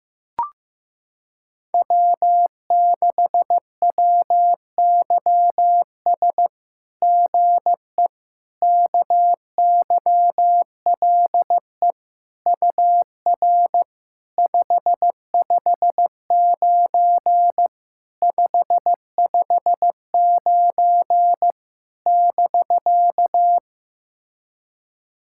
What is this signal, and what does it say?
Morse code, heard as a keyed tone, about 15 wpm, 700 Hz. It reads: W6WYS GE KYLE UR 559 559 <BK>
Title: SOTA QSO Practice 15wpm